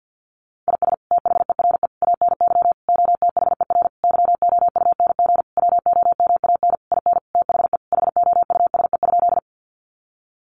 Morse code, heard as a keyed tone, about 50 wpm, 700 Hz. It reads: IS THERE ANY OTHER YOUNG WOMAN IN THE HOUSE?
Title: Sentences from Top 1100 Words 50wpm